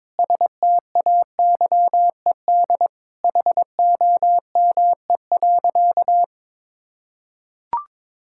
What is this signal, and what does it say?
Morse code, heard as a keyed tone, about 22 wpm, 700 Hz. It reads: STAYED HOME.
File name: practice / 22wpm / Sentences with 4-Word Phrases - 22wpm